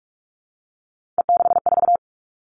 Morse code, heard as a keyed tone, about 46 wpm, 700 Hz. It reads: E64
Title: Three Challenging Letter-Number Combinations 45wpm